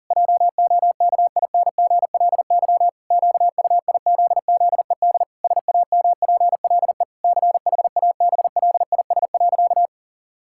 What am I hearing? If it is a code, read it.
Morse code, heard as a keyed tone, about 40 wpm, 700 Hz. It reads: JOKINGLY QUIZZED SAMPLE CHABLIS.
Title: Pangram Challenge v2 - 40wpm